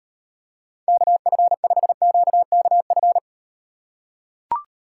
Morse code, heard as a keyed tone, about 38 wpm, 700 Hz. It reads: KF5QKF